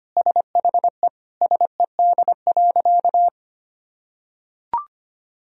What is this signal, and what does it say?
Morse code, heard as a keyed tone, about 25 wpm, 700 Hz. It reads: SHE SED.
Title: Sentences from Top 100 Words and Abbreviations 25wpm